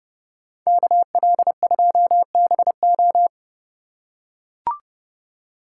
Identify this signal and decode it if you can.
Morse code, heard as a keyed tone, about 30 wpm, 700 Hz. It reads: KL2BO